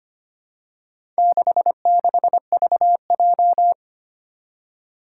Morse code, heard as a keyed tone, about 25 wpm, 700 Hz. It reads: 66VJ
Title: Four Challenging Letter-Number Combinations 25wpm